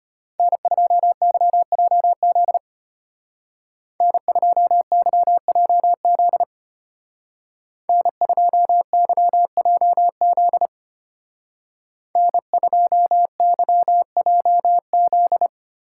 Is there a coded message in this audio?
Yes: Morse code from 0.4 to 15.5 s, heard as a keyed tone, about 32 wpm, 700 Hz. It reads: N2YJZ N2YJZ N2YJZ N2YJZ